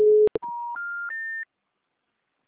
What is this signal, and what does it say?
Special information tone.